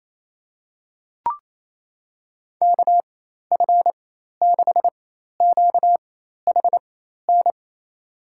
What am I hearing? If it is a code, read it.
Morse code, heard as a keyed tone, about 28 wpm, spaced as for 13 wpm, 700 Hz. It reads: KF6QHN
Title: USA Call Signs - 4x Character Spacing - 28wpm